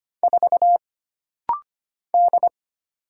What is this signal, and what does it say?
Morse code, heard as a keyed tone, about 25 wpm, 700 Hz. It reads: 4 D
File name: Single Character (Letter:Number:Punctuation:Prosign) - Rapid-Fire - 25wpm